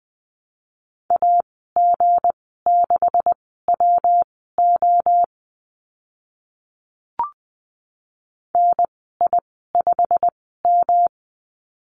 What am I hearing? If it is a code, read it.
Morse code, heard as a keyed tone, about 20 wpm, spaced as for 14 wpm, 700 Hz. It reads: AG6WO NI5M
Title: 161 Lesson 26g - Letter g - Call Signs 20wpm